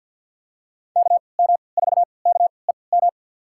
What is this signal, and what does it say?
Morse code, heard as a keyed tone, about 47 wpm, spaced as for 27 wpm, 690 Hz. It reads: KM4KEM
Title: USA Call Signs - 6 Characters - 3x Character Spacing - 50wpm